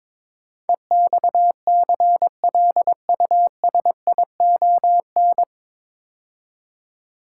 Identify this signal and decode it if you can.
Morse code, heard as a keyed tone, about 22 wpm, 700 Hz. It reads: EXCLUSION
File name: 9 Letter Words - First 3 Letters Given - 22wpm